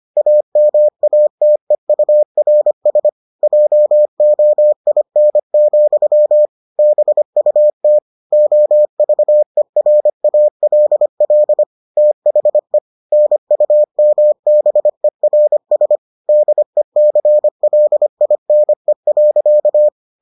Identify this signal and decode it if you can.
Morse code, heard as a keyed tone, about 25 wpm, 600 Hz. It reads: AMATEURS JOIN, BUT OVERALL THE NUMBERS DECLINE.